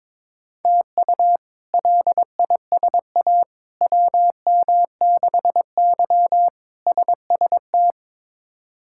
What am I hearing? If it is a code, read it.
Morse code, heard as a keyed tone, about 22 wpm, 700 Hz. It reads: TU LISA WM6Y SST
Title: K1USN SST QSO Practice - 22wpm